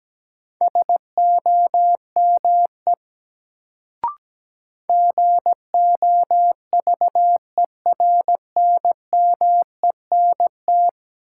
Morse code, heard as a keyed tone, about 17 wpm, 700 Hz. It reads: SOME GOVERNMENT
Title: Sets of 2 Words - New Word Drill - 17wpm